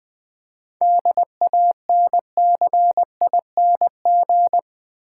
Morse code, heard as a keyed tone, about 21 wpm, 700 Hz. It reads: DANCING
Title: 7 Letter Words - First 3 Letters Given - 20wpm